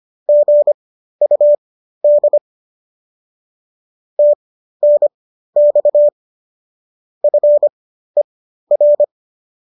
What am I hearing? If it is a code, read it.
Morse code, heard as a keyed tone, about 25 wpm, spaced as for 13 wpm, 590 Hz. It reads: GUD TNX FER